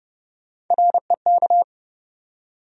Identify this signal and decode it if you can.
Morse code, heard as a keyed tone, about 30 wpm, 700 Hz. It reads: REK
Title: USA Call Sign Postfixes - 3 Letters 30wpm